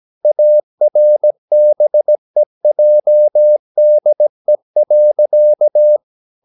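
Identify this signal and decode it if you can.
Morse code, heard as a keyed tone, about 17 wpm, 600 Hz. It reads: ARBEJDE.
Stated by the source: Slow (QRS) Morse practice at 17 wpm, from CW med Gnister.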